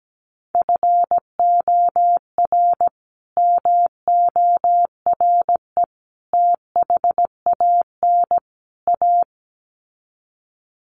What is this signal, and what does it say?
Morse code, heard as a keyed tone, about 17 wpm, 700 Hz. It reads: FOR MORE THAN A
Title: Sets of 4 Words 17wpm